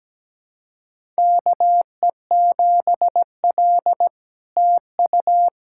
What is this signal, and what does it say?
Morse code, heard as a keyed tone, about 17 wpm, 700 Hz. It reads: KE7L TU
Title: SOTA QSO Practice 17wpm